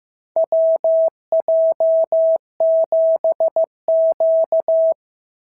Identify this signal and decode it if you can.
Morse code, heard as a keyed tone, about 15 wpm, 650 Hz. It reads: WJ7Q
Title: SOTA QSO Practice 15wpm